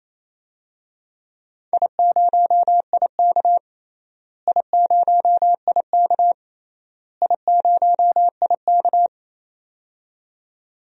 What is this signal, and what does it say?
Morse code, heard as a keyed tone, about 27 wpm, 700 Hz. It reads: I0IK I0IK I0IK
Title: DXCC-100 Call Signs - 4 Characters - Repeated 3x - 28wpm